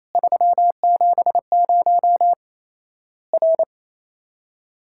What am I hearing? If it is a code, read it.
Morse code, heard as a keyed tone, about 28 wpm, 700 Hz. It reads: 370 R
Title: Worked All Europe DX Contest - QTC Exchange - 28wpm